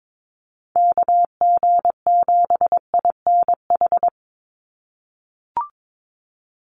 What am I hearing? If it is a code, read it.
Morse code, heard as a keyed tone, about 22 wpm, 700 Hz. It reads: KG7INH
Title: USA Call Signs 22wpm